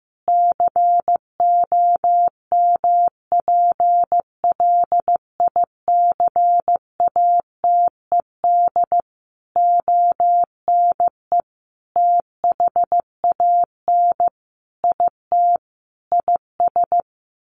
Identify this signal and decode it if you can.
Morse code, heard as a keyed tone, about 15 wpm, 700 Hz. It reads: COMPLICATED ONE THAN IT IS
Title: Sentences from Top 1200 Words 15wpm